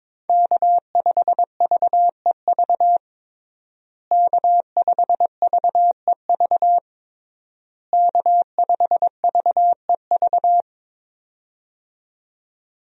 Morse code, heard as a keyed tone, about 22 wpm, 700 Hz. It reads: K5VEV K5VEV K5VEV
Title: USA Call Signs - 5 Characters - Repeated 3x - 22wpm